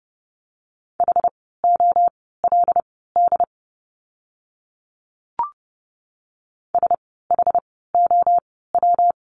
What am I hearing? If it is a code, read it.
Morse code, heard as a keyed tone, about 30 wpm, spaced as for 17 wpm, 700 Hz. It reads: HOLD SHOW
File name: course / 30:10wpm / 115 Lesson 19f - Letter w - Words 30wpm